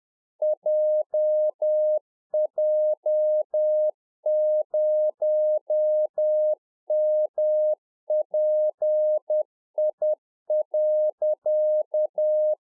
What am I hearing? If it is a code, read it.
Morse code, heard as a keyed tone, about 10 wpm, 610 Hz. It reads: JJ0MPI.
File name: 2011-04-mp3 / RICK JJ0MPI 10WPM